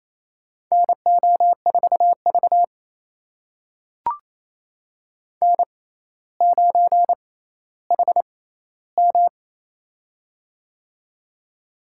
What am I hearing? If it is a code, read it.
Morse code, heard as a keyed tone, about 28 wpm, spaced as for 14 wpm, 700 Hz. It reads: NO4V N9HM
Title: USA Call Signs - 4 Characters - 6x Character Spacing - 28wpm